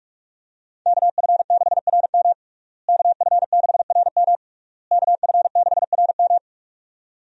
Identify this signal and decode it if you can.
Morse code, heard as a keyed tone, about 46 wpm, 700 Hz. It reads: KF6RM KF6RM KF6RM
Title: USA Call Signs - 5 Characters - Repeated 3x - 45wpm